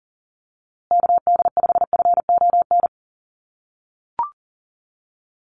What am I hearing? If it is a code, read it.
Morse code, heard as a keyed tone, about 40 wpm, 700 Hz. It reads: KD5FON
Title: USA Call Signs Repeated 3x 40wpm